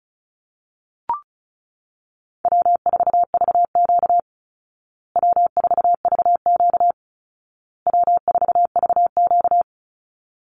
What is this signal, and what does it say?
Morse code, heard as a keyed tone, about 34 wpm, 700 Hz. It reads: W4VQ W4VQ W4VQ
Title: Call Signs Repeated 2x - 35wpm